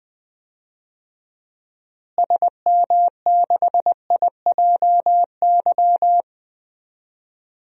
Morse code, heard as a keyed tone, about 20 wpm, 700 Hz. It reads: SM6IJY
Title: DXCC-100 Call Signs - 6 Characters - Repeated 2x - 20wpm